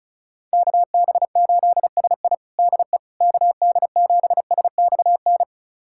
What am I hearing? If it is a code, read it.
Morse code, heard as a keyed tone, about 36 wpm, 700 Hz. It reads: KB8SI DE KD7SXN
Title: USA Callsign DE Callsign 35wpm